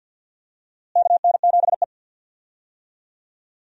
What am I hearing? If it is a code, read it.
Morse code, heard as a keyed tone, about 47 wpm, 700 Hz. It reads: KN7E
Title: USA Call Signs - 4 Characters - Repeated 3x - 50wpm